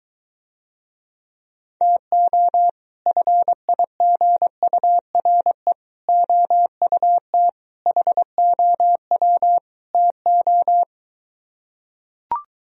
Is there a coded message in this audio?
Yes: Morse code from 1.8 to 10.8 s, heard as a keyed tone, about 23 wpm, 700 Hz. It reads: TO FIGURE OUT HOW TO